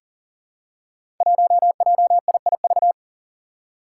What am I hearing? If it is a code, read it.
Morse code, heard as a keyed tone, about 40 wpm, 700 Hz. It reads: 1JIIV
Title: Five Challenging Letter-Number Combinations 40wpm